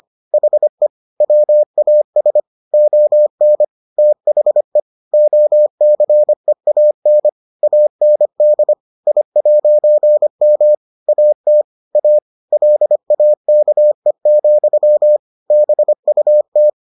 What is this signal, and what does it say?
Morse code, heard as a keyed tone, about 25 wpm, 600 Hz. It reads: HE WAS ON THE OCEAN AND I'M AT A LAKE, BUT